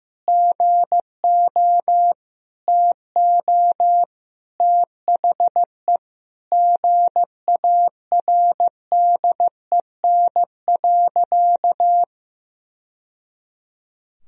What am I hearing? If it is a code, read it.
Morse code, heard as a keyed tone, about 16 wpm, 700 Hz. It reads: GO TO THE GARDEN.
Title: Sentences from Top 1000 Words - Review Part 4 - 15wpm